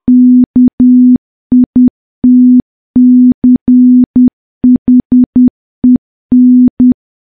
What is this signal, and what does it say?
Morse code, heard as a keyed tone, about 10 wpm, 250 Hz. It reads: KITCHEN
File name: morse / morse-training-10WPM